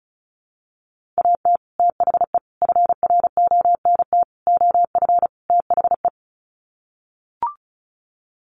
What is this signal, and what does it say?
Morse code, heard as a keyed tone, about 34 wpm, 700 Hz. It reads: AT THE FRONT OF THE
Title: Sets of 5 Words 35wpm